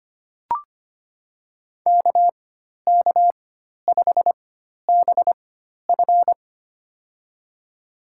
Morse code, heard as a keyed tone, about 25 wpm, spaced as for 12 wpm, 700 Hz. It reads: KK5BF
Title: USA Call Signs - 5 Characters - 4x Character Spacing - 25wpm